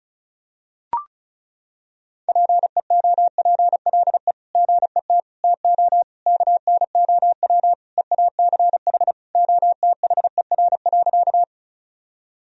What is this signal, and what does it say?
Morse code, heard as a keyed tone, about 36 wpm, 700 Hz. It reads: PEOPLE GET TO KNOW EACH OTHER.
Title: Sentences from Top 1000 Words - Review Part 1 - 35wpm